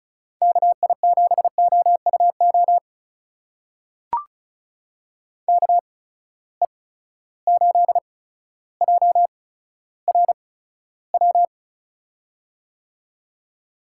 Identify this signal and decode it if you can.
Morse code, heard as a keyed tone, about 36 wpm, spaced as for 15 wpm, 700 Hz. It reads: KI7OUO KE8JRW
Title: USA Call Signs - 6 Characters - 8x Character Spacing - 35wpm